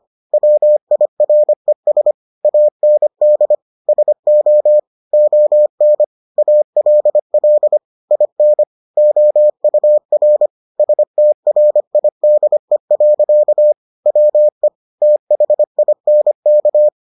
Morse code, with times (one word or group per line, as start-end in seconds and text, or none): 0.3-2.1 WIRES
2.5-3.5 AND
3.9-4.8 SO
5.1-6.0 ON
6.4-7.8 ALL
8.1-8.6 IN
9.0-10.5 OUR
10.8-13.7 STRIDE.
14.1-14.7 WE
15.0-16.9 THINK